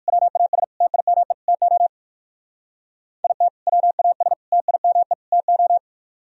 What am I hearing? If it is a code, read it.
Morse code, heard as a keyed tone, about 53 wpm, 700 Hz. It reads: WAS TIME TO IT WAS TIME TO
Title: Speed-Racing - Sets of 4 Words 45wpm